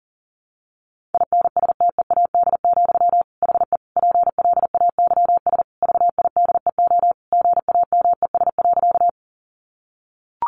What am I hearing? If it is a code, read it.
Morse code, heard as a keyed tone, about 40 wpm, 700 Hz. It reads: INSTEAD, HE PLAYS VIDEO GAMES.
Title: Additional Easy Phrases 40wpm